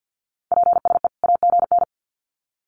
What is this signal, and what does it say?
Morse code, heard as a keyed tone, about 50 wpm, 700 Hz. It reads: PSE AGN
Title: Common QSO Phrases 50wpm